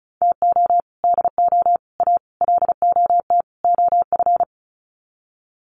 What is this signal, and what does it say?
Morse code, heard as a keyed tone, about 34 wpm, 700 Hz. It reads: TO DO A LOT OF
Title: Sets of 5 Words Encore 35wpm